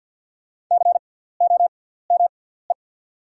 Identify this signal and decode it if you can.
Morse code, heard as a keyed tone, about 48 wpm, spaced as for 17 wpm, 690 Hz. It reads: COME
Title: 6x Character Spacing - Top 200 Words - 50wpm